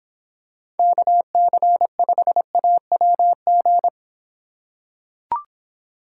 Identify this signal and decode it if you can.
Morse code, heard as a keyed tone, about 26 wpm, 700 Hz. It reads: KC5AWG